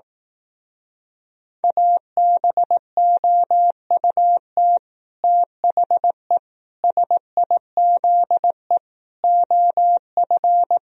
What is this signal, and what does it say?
Morse code, heard as a keyed tone, about 18 wpm, 700 Hz. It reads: ABOUT THE SIZE OF
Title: Speed-Racing - Sets of 5 Words 15wpm